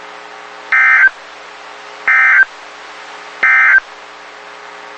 Emergency Alert System (SAME) tones, heard as harsh buzzing bursts. It is heard from 0.6 to 3.9 s.
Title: WXK91 NOAA Weather Radio